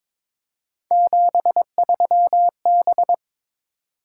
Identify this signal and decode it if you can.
Morse code, heard as a keyed tone, about 23 wpm, 700 Hz. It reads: 73B
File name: practice / 22wpm / Three Letter-Number Combinations - 22wpm